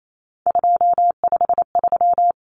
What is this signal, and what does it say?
Morse code, heard as a keyed tone, about 28 wpm, 700 Hz. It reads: 253
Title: Numbers - 3-Digit 28wpm